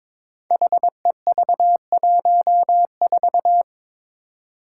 Morse code, heard as a keyed tone, about 21 wpm, 700 Hz. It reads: HEV14
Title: Five Challenging Letter-Number Combinations 22wpm